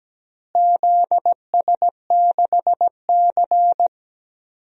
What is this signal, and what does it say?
Morse code, heard as a keyed tone, about 17 wpm, 700 Hz. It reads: ZS6C
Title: DXCC-100 Call Signs - 4 Characters - Repeated 3x - 17wpm